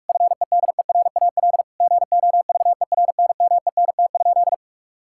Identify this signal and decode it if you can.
Morse code, heard as a keyed tone, about 46 wpm, 700 Hz. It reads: FEDERAL GOVERNMENT?
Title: Sentences from Top 1000 Words - Review Part 5 - 45wpm